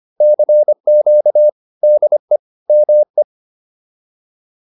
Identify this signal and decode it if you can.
Morse code, heard as a keyed tone, about 25 wpm, 600 Hz. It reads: CQ DE ME